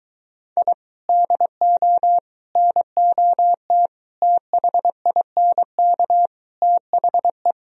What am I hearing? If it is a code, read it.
Morse code, heard as a keyed tone, about 23 wpm, 700 Hz. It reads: I DO NOT THINK THE